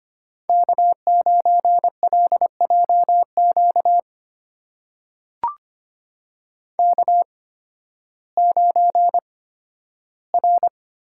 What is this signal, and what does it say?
Morse code, heard as a keyed tone, about 25 wpm, spaced as for 13 wpm, 700 Hz. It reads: K9LJQ K9R
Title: USA Call Signs - 8x Character Spacing - 25wpm